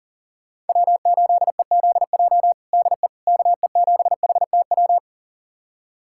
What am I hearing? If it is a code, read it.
Morse code, heard as a keyed tone, about 40 wpm, 700 Hz. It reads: W8EZJ DE KE7HTW
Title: USA Callsign DE Callsign 40wpm